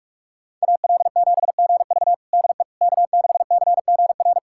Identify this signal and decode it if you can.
Morse code, heard as a keyed tone, about 42 wpm, 700 Hz. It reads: AL7GV DE K6CGR
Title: USA Callsign DE Callsign - 5 Characters - 45wpm